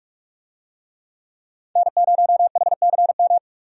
Morse code, heard as a keyed tone, about 43 wpm, 700 Hz. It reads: N0HCM
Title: USA Call Signs - 5 Characters - 45wpm